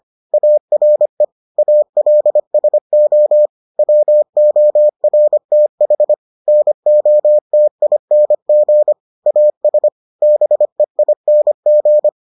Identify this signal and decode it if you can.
Morse code, heard as a keyed tone, about 25 wpm, 600 Hz. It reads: ARE ALSO WORTH NOTING AS BEING